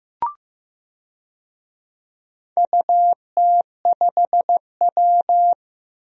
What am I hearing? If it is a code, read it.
Morse code, heard as a keyed tone, about 15 wpm, 700 Hz. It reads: UT5W
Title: DXCC-100 Call Signs - 4 Characters - Repeated 2x - 15wpm